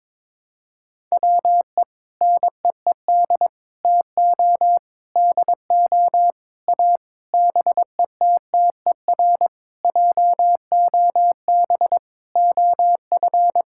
Morse code, heard as a keyed tone, about 22 wpm, 700 Hz. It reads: WE NEED TO DO A BETTER JOB OF
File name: Sentences with 7-Word Phrases - 22wpm